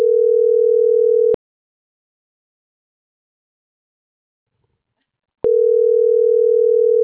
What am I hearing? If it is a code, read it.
Ringback tone (North America), 440 + 480 Hz.